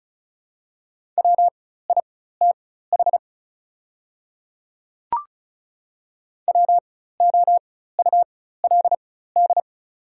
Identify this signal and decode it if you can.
Morse code, heard as a keyed tone, about 34 wpm, spaced as for 16 wpm, 690 Hz. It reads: WITH WOULD
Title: 4x Character Spacing - Top 500 Words - 35wpm